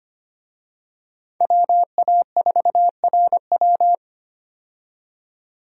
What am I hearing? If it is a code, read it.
Morse code, heard as a keyed tone, about 25 wpm, 700 Hz. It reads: WA4RW
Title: USA Call Signs - 5 Characters - 25wpm